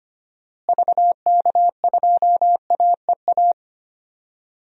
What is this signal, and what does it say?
Morse code, heard as a keyed tone, about 25 wpm, 700 Hz. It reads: VK2AEA
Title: DXCC-100 Call Signs - 6 Characters - Repeated 2x - 25wpm